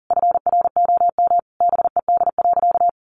Morse code, heard as a keyed tone, about 40 wpm, 700 Hz. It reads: FROM BED.
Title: Sentences from Top 900 Words 40wpm